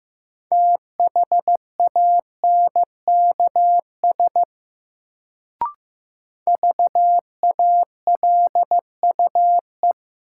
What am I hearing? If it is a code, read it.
Morse code, heard as a keyed tone, about 15 wpm, 700 Hz. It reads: THANKS VALUE
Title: Sentences from Top 600 Words - New Word Drill - 15wpm